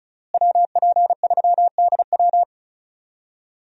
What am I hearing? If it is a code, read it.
Morse code, heard as a keyed tone, about 34 wpm, 700 Hz. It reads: WP3DW
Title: USA Call Signs - 5 Characters - Repeated 2x - 35wpm